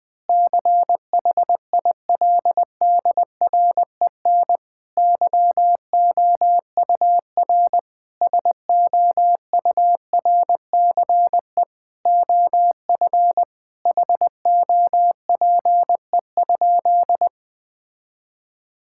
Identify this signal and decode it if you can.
Morse code, heard as a keyed tone, about 19 wpm, 700 Hz. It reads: CHILDREN YOUR SOURCE OF HOPE?
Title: Sentences from Top 1000 Words - Review Part 5 - 20wpm